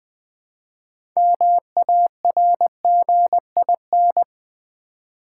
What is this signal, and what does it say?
Morse code, heard as a keyed tone, about 20 wpm, 700 Hz. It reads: MARGIN
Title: Build A Word - 2 Letters At A Time - 8-Letter Words - 20wpm